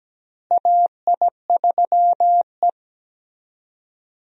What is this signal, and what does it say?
Morse code, heard as a keyed tone, about 17 wpm, 700 Hz. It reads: AI3E
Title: USA Call Signs - 4 Characters - Repeated 3x - 17wpm